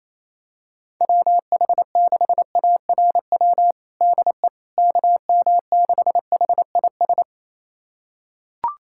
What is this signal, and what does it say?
Morse code, heard as a keyed tone, about 28 wpm, 700 Hz. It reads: WH6ARW DE KM6HIS
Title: USA Callsign DE Callsign - 6 Characters - 28wpm